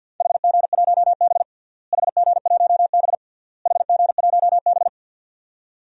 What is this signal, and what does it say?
Morse code, heard as a keyed tone, about 50 wpm, 700 Hz. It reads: HG1B HG1B HG1B
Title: DXCC-100 Call Signs - 4 Characters - Repeated 3x - 50wpm